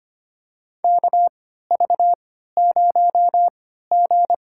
Morse code, heard as a keyed tone, about 25 wpm, spaced as for 14 wpm, 700 Hz. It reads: KV0G
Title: USA Call Signs - 4 Characters - 3x Character Spacing - 25wpm